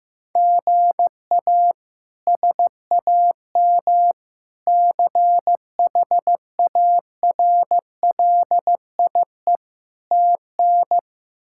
Morse code, heard as a keyed tone, about 15 wpm, 700 Hz. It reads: GA SAM CHARLIE TN
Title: K1USN SST QSO Practice - 15wpm